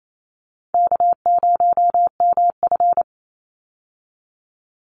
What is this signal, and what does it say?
Morse code, heard as a keyed tone, about 28 wpm, 700 Hz. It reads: K0MF
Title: USA Call Signs 28wpm